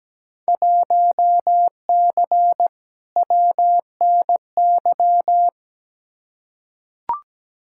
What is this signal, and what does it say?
Morse code, heard as a keyed tone, about 17 wpm, 700 Hz. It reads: 1C WNY
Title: Field Day Exchanges Encore - 17wpm